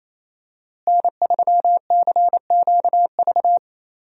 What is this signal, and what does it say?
Morse code, heard as a keyed tone, about 28 wpm, 700 Hz. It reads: N3CQV
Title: USA Call Signs - 5 Characters - Repeated 2x - 28wpm